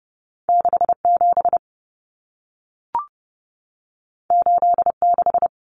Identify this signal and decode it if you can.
Morse code, heard as a keyed tone, about 30 wpm, 700 Hz. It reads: 67 86
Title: Numbers - 2-Digit 30wpm